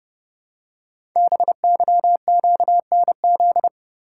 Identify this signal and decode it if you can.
Morse code, heard as a keyed tone, about 30 wpm, 700 Hz. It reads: BYQNZ
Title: Code Groups - 5 Letters - 30wpm